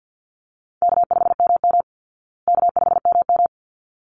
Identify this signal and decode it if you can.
Morse code, heard as a keyed tone, about 50 wpm, 700 Hz. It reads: K5MM K5MM
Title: Easy Call Signs Repeated 2x - 50wpm